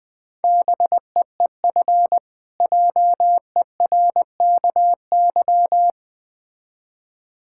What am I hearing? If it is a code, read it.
Morse code, heard as a keyed tone, about 21 wpm, 700 Hz. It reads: BEEF JERKY
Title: Difficult Words with Hints - 20wpm